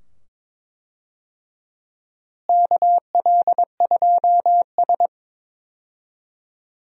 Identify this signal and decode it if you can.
Morse code, heard as a keyed tone, about 21 wpm, 700 Hz. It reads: KL2S
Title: USA Call Signs - 4 Characters - Repeated 2x - 22wpm